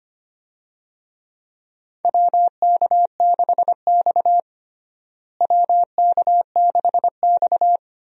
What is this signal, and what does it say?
Morse code, heard as a keyed tone, about 25 wpm, 700 Hz. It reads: WK6X WK6X
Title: USA Call Signs - 4 Characters - Repeated 2x - 25wpm